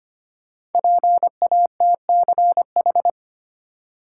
Morse code, heard as a keyed tone, about 25 wpm, 700 Hz. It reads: PATCH
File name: 5 Letter Words - First 3 Letters Given - 25wpm